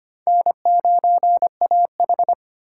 Morse code, heard as a keyed tone, about 25 wpm, 700 Hz. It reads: N9AH
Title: USA Call Signs - 4 Characters - 2x Character Spacing - 25wpm